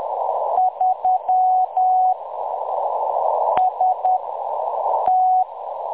Morse code, heard as a keyed tone, about 10 wpm, 740 Hz, with noise. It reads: 3 S T
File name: M23-ST3- -0858utc-10184khz